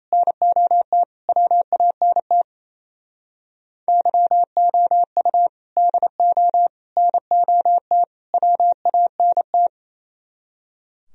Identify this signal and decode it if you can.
Morse code, heard as a keyed tone, about 36 wpm, 700 Hz. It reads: NOT WANT YOU DO NOT WANT